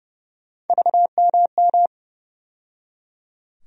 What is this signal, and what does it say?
Morse code, heard as a keyed tone, about 32 wpm, 700 Hz. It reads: VMM
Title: USA Call Sign Postfixes - 3 Letters 30wpm